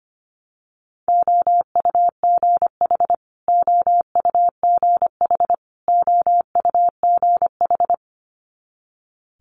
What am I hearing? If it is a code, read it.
Morse code, heard as a keyed tone, about 25 wpm, 700 Hz. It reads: OUGH OUGH OUGH
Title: Multi-letter Phonemes Repeated 3x 25wpm